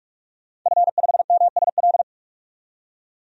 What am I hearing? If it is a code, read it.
Morse code, heard as a keyed tone, about 46 wpm, 700 Hz. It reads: F5MSL
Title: DXCC-100 Call Signs - 5 Characters - 45wpm